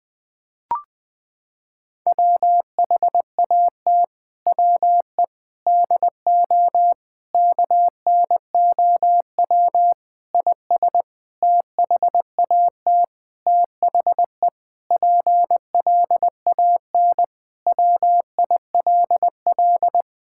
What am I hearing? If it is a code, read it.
Morse code, heard as a keyed tone, about 20 wpm, 700 Hz. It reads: WHAT WE DO KNOW IS THAT THE PLAN WILL
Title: Sentences with 7-Word Phrases - 20wpm